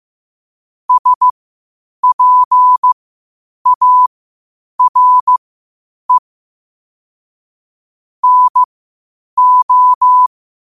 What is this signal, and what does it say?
Morse code, heard as a keyed tone, about 15 wpm, spaced as for 8 wpm, 1000 Hz. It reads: SPARE NO
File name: qotdfiles / 8wpm